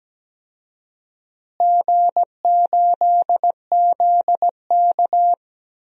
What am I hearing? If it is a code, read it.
Morse code, heard as a keyed tone, about 17 wpm, 700 Hz. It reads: G8ZK
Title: DXCC-100 Call Signs - 4 Characters - Repeated 2x - 17wpm